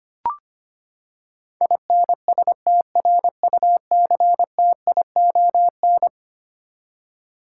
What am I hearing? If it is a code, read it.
Morse code, heard as a keyed tone, about 25 wpm, 700 Hz. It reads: INSTRUCTION
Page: morse-code-ninja-practice-25wpm